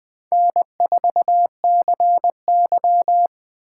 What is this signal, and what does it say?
Morse code, heard as a keyed tone, about 20 wpm, 700 Hz. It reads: N4CY
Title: Speed Racing - USA 1x2 Call Signs 17wpm